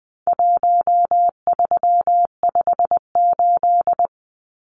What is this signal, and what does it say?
Morse code, heard as a keyed tone, about 20 wpm, 700 Hz. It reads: 1358
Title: Numbers - Contest Style - 4-Digit 20wpm